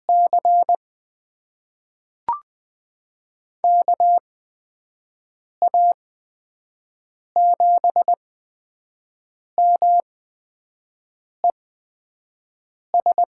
Morse code, heard as a keyed tone, about 21 wpm, spaced as for 5 wpm, 700 Hz. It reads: C KA7MES
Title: USA Call Signs - 8x Character Spacing - 20wpm